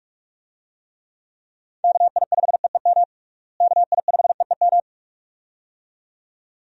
Morse code, heard as a keyed tone, about 46 wpm, 700 Hz. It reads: KI5EEM KI5EEM
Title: USA Call Signs - 6 Characters - Repeated 2x - 45wpm